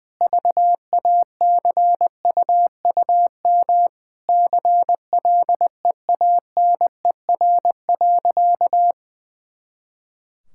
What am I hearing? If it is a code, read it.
Morse code, heard as a keyed tone, about 20 wpm, 700 Hz. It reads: VACUUM CLEANER.
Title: Pangram Challenge v2 - 20wpm